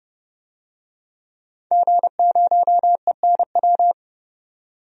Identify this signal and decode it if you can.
Morse code, heard as a keyed tone, about 30 wpm, 700 Hz. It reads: G0ENW
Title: DXCC-100 Call Signs - 5 Characters - 30wpm